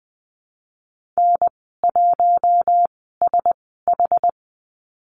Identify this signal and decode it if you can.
Morse code, heard as a keyed tone, about 20 wpm, spaced as for 15 wpm, 700 Hz. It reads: N1SH
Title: 082 Lesson 14g - Number 2 - Call Signs 20wpm